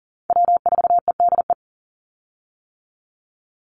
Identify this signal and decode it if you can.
Morse code, heard as a keyed tone, about 40 wpm, 700 Hz. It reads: W4EDE
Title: Call Signs - 40wpm